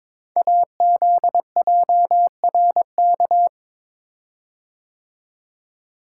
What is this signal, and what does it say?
Morse code, heard as a keyed tone, about 23 wpm, 700 Hz. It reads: AZJRK
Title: Code Groups - 5 Letter:Numbers - 22wpm